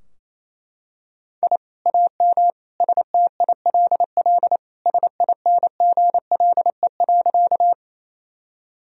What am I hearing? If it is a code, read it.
Morse code, heard as a keyed tone, about 28 wpm, 700 Hz. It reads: I AM STILL SINGLE.
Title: Sentences from Top 1000 Words - Review Part 4 - 28wpm